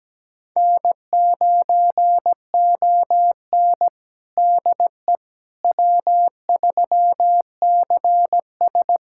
Morse code, heard as a keyed tone, about 17 wpm, 700 Hz. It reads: N9ON DE W3CS
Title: USA Callsign DE Callsign - 4 Characters - 17wpm